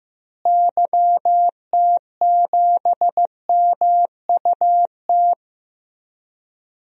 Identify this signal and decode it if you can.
Morse code, heard as a keyed tone, about 16 wpm, 700 Hz. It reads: YT7MUT
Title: DXCC-100 Call Signs - 6 Characters - Repeated 3x - 15wpm